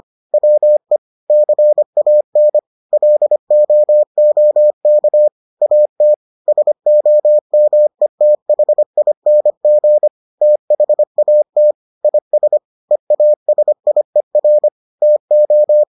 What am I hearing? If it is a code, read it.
Morse code, heard as a keyed tone, about 25 wpm, 600 Hz. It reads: WE CAN LOOK AT SOMETHING THAT IS EASIER TO